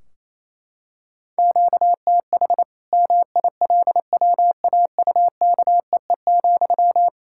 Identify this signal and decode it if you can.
Morse code, heard as a keyed tone, about 28 wpm, 700 Hz. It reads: QTH MILWAUKEE,
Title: K1USN SST QSO Practice 28wpm